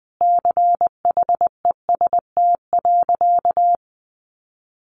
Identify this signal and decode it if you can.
Morse code, heard as a keyed tone, about 20 wpm, 700 Hz. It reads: CHEST.
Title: A Sound Of Thunder 20wpm